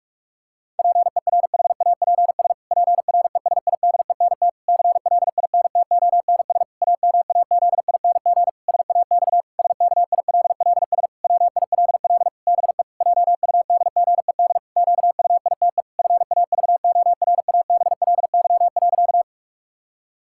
Morse code, heard as a keyed tone, about 44 wpm, 700 Hz. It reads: PERHAPS PRESIDENT CLINTONS AMAZING SAX SKILLS WILL BE JUDGED QUITE FAVORABLY.
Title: Pangram Challenge v2 - 45wpm